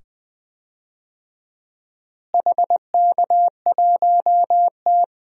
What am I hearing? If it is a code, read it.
Morse code, heard as a keyed tone, about 20 wpm, 700 Hz. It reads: HK1T
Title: DXCC-100 Call Signs - 4 Characters - Repeated 2x - 20wpm